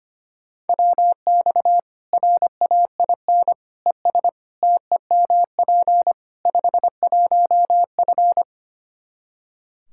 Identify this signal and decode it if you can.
Morse code, heard as a keyed tone, about 25 wpm, 700 Hz. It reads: WX RAIN ES TEMP 51F
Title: Practice QSO - Exchange 2 - Info - 25wpm